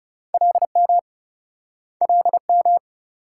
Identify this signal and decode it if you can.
Morse code, heard as a keyed tone, about 35 wpm, 700 Hz. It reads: LM LM
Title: Speed-Racing Multi-letter Phonemes - 30wpm